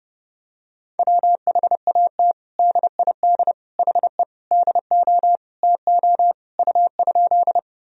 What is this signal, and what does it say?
Morse code, heard as a keyed tone, about 30 wpm, 700 Hz. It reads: WHAT DID HE DO TO U?
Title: Sentences from Top 100 Words and Abbreviations 30wpm